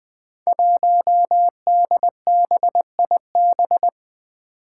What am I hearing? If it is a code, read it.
Morse code, heard as a keyed tone, about 21 wpm, 700 Hz. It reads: 1DBIB
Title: Five Challenging Letter-Number Combinations 20wpm